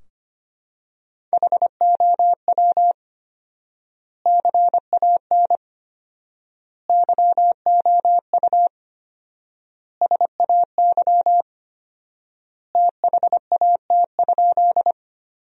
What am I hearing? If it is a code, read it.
Morse code, heard as a keyed tone, about 25 wpm, 700 Hz. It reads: HOW CAN YOU SAY THAT?
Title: 4x Word-Spacing - Sentences from Top 200 Words - 25wpm